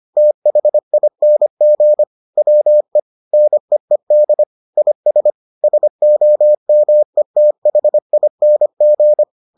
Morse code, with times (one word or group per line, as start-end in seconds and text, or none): 0.2-2.0 THING
2.4-3.0 WE
3.3-4.4 NEED
4.8-5.3 IS
5.6-9.2 SOMETHING